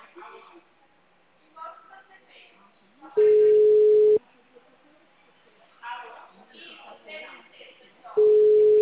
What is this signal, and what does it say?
Ringback tone (Europe, 425 Hz), with speech.
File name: out- -2013- - - .31099